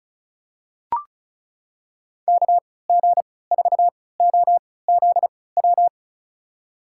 Morse code, heard as a keyed tone, about 34 wpm, spaced as for 20 wpm, 700 Hz. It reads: KG4OZW
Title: USA Call Signs - 6 Characters - 3x Character Spacing - 35wpm